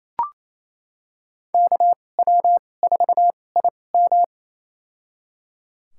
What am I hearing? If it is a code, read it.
Morse code, heard as a keyed tone, about 28 wpm, spaced as for 20 wpm, 700 Hz. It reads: KW4IM
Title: USA Call Signs - 5 Characters - 2x Character Spacing - 28wpm